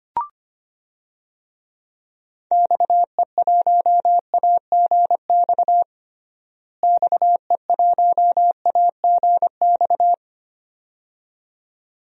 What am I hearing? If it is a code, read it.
Morse code, heard as a keyed tone, about 25 wpm, 700 Hz. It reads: XE1AGX XE1AGX
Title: DXCC-100 Call Signs - 6 Characters - Repeated 2x - 25wpm